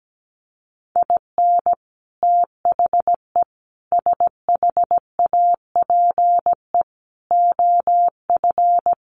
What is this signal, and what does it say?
Morse code, heard as a keyed tone, about 17 wpm, 700 Hz. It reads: IN THE SHAPE OF
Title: Sets of 5 Words Encore 17wpm